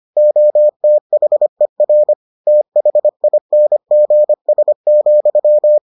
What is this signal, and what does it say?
Morse code, heard as a keyed tone, about 25 wpm, 600 Hz. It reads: OTHER THINGS,